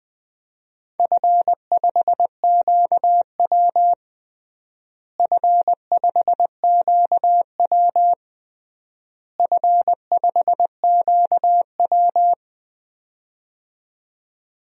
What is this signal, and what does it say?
Morse code, heard as a keyed tone, about 20 wpm, 700 Hz. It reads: F5QW F5QW F5QW
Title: DXCC-100 Call Signs - 4 Characters - Repeated 3x - 20wpm